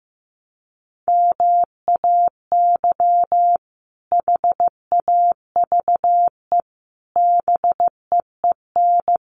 Morse code, heard as a keyed tone, about 15 wpm, 700 Hz. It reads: MAY HAVE BEEN
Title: Sets of 3 Words 15wpm